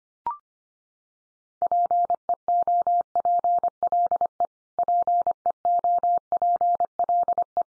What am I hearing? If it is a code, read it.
Morse code, heard as a keyed tone, about 25 wpm, 700 Hz. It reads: PEOPLE PEOPLE
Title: Top 100 Words 25wpm x3